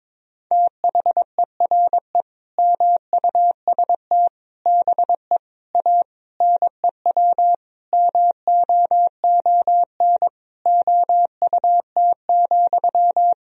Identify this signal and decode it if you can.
Morse code, heard as a keyed tone, about 22 wpm, 700 Hz. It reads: THERE MUST BE A NEW MOON OUT,
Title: Sentences from Top 1000 Words - Review Part 3 - 22wpm